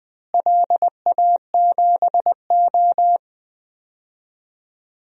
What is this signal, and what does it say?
Morse code, heard as a keyed tone, about 20 wpm, 700 Hz. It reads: LA7O
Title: DXCC-100 Call Signs - 4 Characters - Repeated 3x - 20wpm